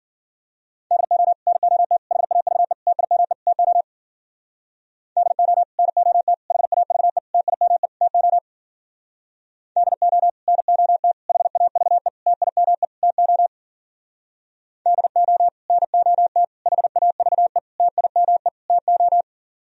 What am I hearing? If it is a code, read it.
Morse code, heard as a keyed tone, about 51 wpm, 700 Hz. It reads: DO NOT HAVE TIME TO DO NOT HAVE TIME TO DO NOT HAVE TIME TO DO NOT HAVE TIME TO